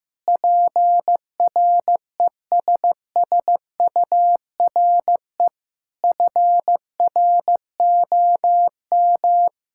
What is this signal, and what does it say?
Morse code, heard as a keyed tone, about 15 wpm, 700 Hz. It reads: PRESSURE FROM
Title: Sentences from Top 1000 Words - Review Part 6 - 15wpm